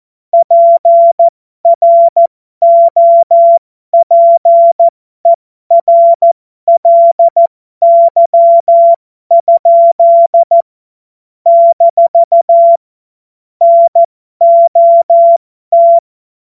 Morse code, heard as a keyed tone, about 14 wpm, spaced as for 12 wpm, 680 Hz. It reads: PROPERLY? - NOT
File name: Morsekurs / Witze